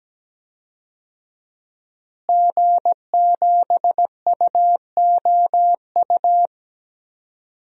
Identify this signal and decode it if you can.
Morse code, heard as a keyed tone, about 17 wpm, 700 Hz. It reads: G7UOU
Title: DXCC-100 Call Signs - 5 Characters - Repeated 3x - 17wpm